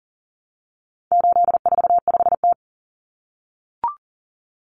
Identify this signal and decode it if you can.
Morse code, heard as a keyed tone, about 40 wpm, 700 Hz. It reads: 845T
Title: Common Cut Numbers - 4-Digit 40wpm